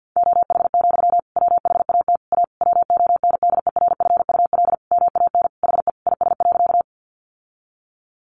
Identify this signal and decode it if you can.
Morse code, heard as a keyed tone, about 50 wpm, 700 Hz. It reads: OH, WHAT A WONDERFUL MAN HE IS.
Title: Sentences from Top 1100 Words 50wpm